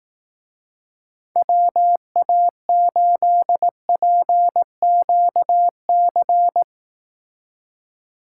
Morse code, heard as a keyed tone, about 18 wpm, 700 Hz. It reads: WA8PQC
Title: Speed Racing - USA 2x3 Call Signs 15wpm